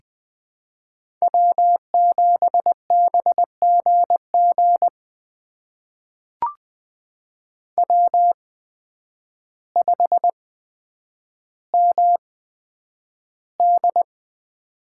Morse code, heard as a keyed tone, about 20 wpm, spaced as for 9 wpm, 700 Hz. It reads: W7BGG W5MD
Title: USA Call Signs - 8x Character Spacing - 20wpm